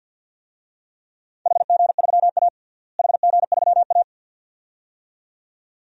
Morse code, heard as a keyed tone, about 47 wpm, 700 Hz. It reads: HG3A HG3A
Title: DXCC-100 Call Signs - 4 Characters - Repeated 2x - 50wpm